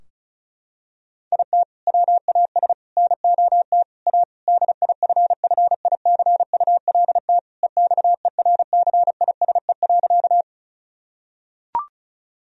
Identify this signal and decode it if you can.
Morse code, heard as a keyed tone, about 36 wpm, 700 Hz. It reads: IT WAS NOT A DIFFICULT EXERCISE.
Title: Sentences from Top 1000 Words - Review Part 4 - 35wpm